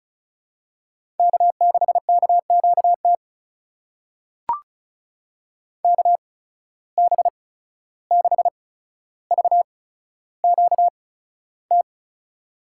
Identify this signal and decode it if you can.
Morse code, heard as a keyed tone, about 36 wpm, spaced as for 14 wpm, 700 Hz. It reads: K6KQT KB6VQT
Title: USA Call Signs - 8x Character Spacing - 35wpm